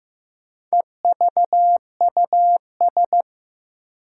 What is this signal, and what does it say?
Morse code, heard as a keyed tone, about 15 wpm, 700 Hz. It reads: EVUS
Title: Four Challenging Letter-Number Combinations 15wpm